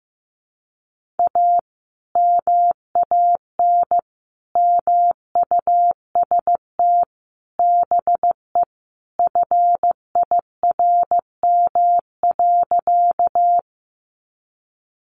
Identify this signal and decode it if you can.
Morse code, heard as a keyed tone, about 15 wpm, 700 Hz. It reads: A MAN MUST BE FIRM.
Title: Sentences from Top 1400 Words 15wpm